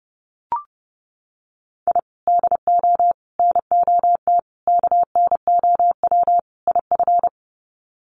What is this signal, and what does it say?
Morse code, heard as a keyed tone, about 30 wpm, 700 Hz. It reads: I DO NOT KNOW IF
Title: Sets of 5 Words 30wpm